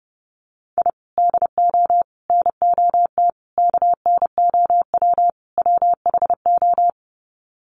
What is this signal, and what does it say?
Morse code, heard as a keyed tone, about 30 wpm, 700 Hz. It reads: I DO NOT KNOW WHO
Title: Sets of 5 Words 30wpm